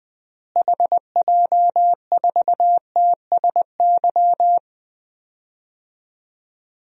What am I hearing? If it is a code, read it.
Morse code, heard as a keyed tone, about 20 wpm, 700 Hz. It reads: HJ4TSY
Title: DXCC-100 Call Signs - 6 Characters - Repeated 3x - 20wpm